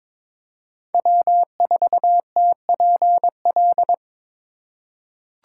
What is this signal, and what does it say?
Morse code, heard as a keyed tone, about 22 wpm, 700 Hz. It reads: W4TPL
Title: USA Call Signs - 5 Characters - 3x Character Spacing - 22wpm